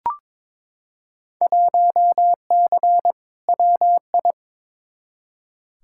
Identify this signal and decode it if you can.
Morse code, heard as a keyed tone, about 22 wpm, 700 Hz. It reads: 1C WI
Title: Field Day Exchanges - 22wpm